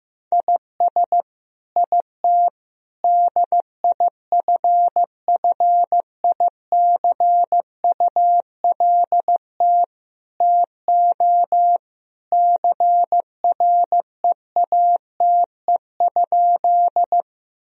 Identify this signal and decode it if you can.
Morse code, heard as a keyed tone, about 15 wpm, 700 Hz. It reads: IS IT DIFFICULT TO CREATE?
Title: Sentences from Top 1000 Words - Review Part 3 - 15wpm